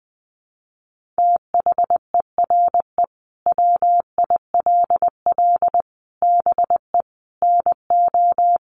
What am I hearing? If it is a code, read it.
Morse code, heard as a keyed tone, about 20 wpm, 700 Hz. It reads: THERE WILL BE NO
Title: Sentences from Top 1500 Words 20wpm